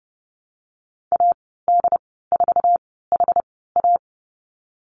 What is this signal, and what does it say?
Morse code, heard as a keyed tone, about 30 wpm, spaced as for 17 wpm, 700 Hz. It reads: AD4HA
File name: Lesson 15g - Number 5 - Call Signs 30wpm